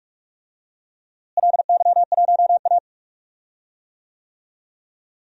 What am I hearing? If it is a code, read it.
Morse code, heard as a keyed tone, about 44 wpm, 700 Hz. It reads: LY1A
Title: DXCC-100 Call Signs - 4 Characters - Repeated 3x - 45wpm